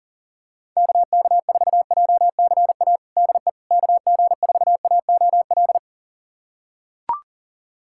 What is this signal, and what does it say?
Morse code, heard as a keyed tone, about 38 wpm, 700 Hz. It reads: KK4JCA DE KG4AOL
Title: USA Callsign DE Callsign - 6 Characters - 40wpm